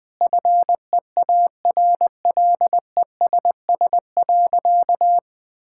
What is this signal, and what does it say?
Morse code, heard as a keyed tone, about 20 wpm, 700 Hz. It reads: FEARLESS.
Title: Sentences with 4-Word Phrases - 20wpm